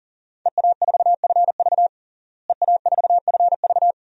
Morse code, heard as a keyed tone, about 40 wpm, 700 Hz. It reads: EA4FV EA4FV
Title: DXCC-100 Call Signs - 5 Characters - Repeated 2x - 40wpm